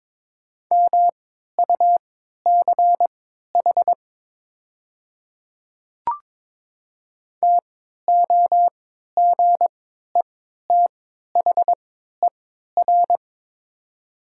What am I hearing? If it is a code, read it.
Morse code, heard as a keyed tone, about 22 wpm, spaced as for 12 wpm, 700 Hz. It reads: MUCH TOGETHER
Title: 3x Character Spacing - Top 500 Words - 22wpm